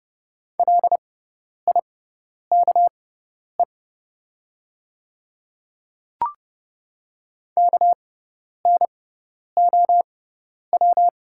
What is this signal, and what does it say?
Morse code, heard as a keyed tone, about 30 wpm, spaced as for 10 wpm, 700 Hz. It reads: LIKE KNOW
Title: 6x Character Spacing - Top 200 Words - 30wpm